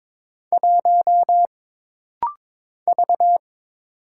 Morse code, heard as a keyed tone, about 23 wpm, 700 Hz. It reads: 1 V
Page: This rapid-fire timing is geared to push your ICR (Instant Character Recognition) skills.